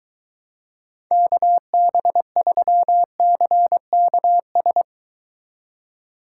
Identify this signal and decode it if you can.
Morse code, heard as a keyed tone, about 23 wpm, 700 Hz. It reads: KB3CKS